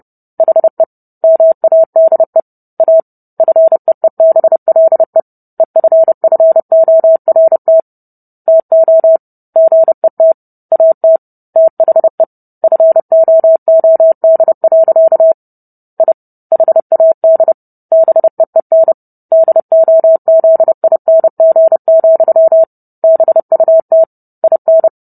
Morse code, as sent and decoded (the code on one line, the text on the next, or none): .... / -- .- -.. / .- / ..-. . -... .-.. / . ..-. ..-. --- .-. - / - --- / --. - / .- - / - .... / ..-. --- --- -.. .-.-.- / .. / .... .- -.. / -... . -. / -.. --- --.. .. -. --. --..-- / -... ..- - / .. -.
HE MADE A FEEBLE EFFORT TO GET AT THE FOOD. I HAD BEEN DOZING, BUT IN